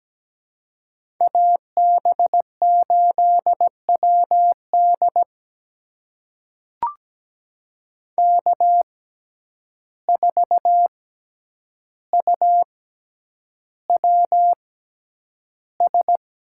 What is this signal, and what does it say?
Morse code, heard as a keyed tone, about 17 wpm, spaced as for 9 wpm, 700 Hz. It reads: AB8WD K4UWS
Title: USA Call Signs - 5 Characters - 6x Character Spacing - 17wpm